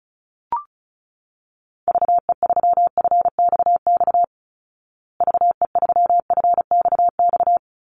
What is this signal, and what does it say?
Morse code, heard as a keyed tone, about 34 wpm, 700 Hz. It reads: VE3FXX VE3FXX
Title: Call Signs Repeated 3x - 35wpm